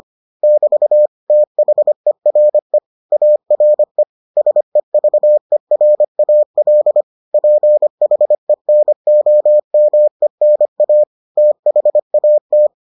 Morse code, heard as a keyed tone, about 25 wpm, 600 Hz. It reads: <BT> THERE ARE SEVERAL PHENOMENA THAT